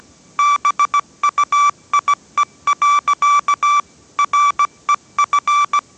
Morse code, heard as a keyed tone, about 17 wpm, 1160 Hz. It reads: BUIE. REF